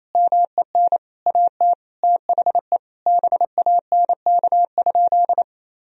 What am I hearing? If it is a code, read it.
Morse code, heard as a keyed tone, about 27 wpm, 700 Hz. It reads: MEN AT THE BANK?
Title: Sentences from Top 1000 Words - Review Part 4 - 28wpm